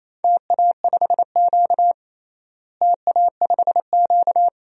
Morse code, heard as a keyed tone, about 28 wpm, 700 Hz. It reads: TA5Q TA5Q
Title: DXCC-100 Call Signs - 4 Characters - Repeated 2x - 28wpm